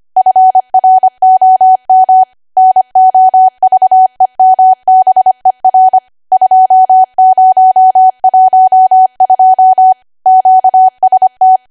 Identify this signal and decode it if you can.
Morse code, heard as a keyed tone, about 25 wpm, 750 Hz. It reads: FROM NOVEMBER 2012 QST
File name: W1AW-Level3/140218 25WPM